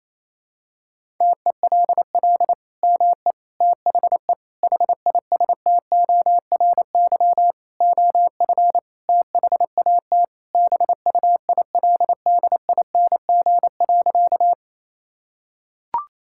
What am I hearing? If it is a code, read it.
Morse code, heard as a keyed tone, about 27 wpm, 700 Hz. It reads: TELL ME THE HISTORY OF THAT BUILDING.
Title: Sentences from Top 1000 Words - Review Part 5 - 28wpm